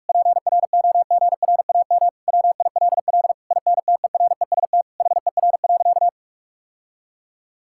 Morse code, heard as a keyed tone, about 46 wpm, 700 Hz. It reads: PROGRAM WILL INTEREST HER.
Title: Sentences from Top 1000 Words - Review Part 5 - 45wpm